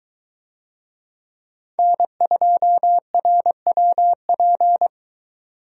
Morse code, heard as a keyed tone, about 21 wpm, 700 Hz. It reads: N2RWP